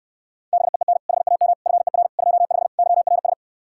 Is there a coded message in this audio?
Yes: Morse code from 0.5 to 3.3 s, heard as a keyed tone, about 68 wpm, 700 Hz. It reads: BEEN HAM FER 25 YRS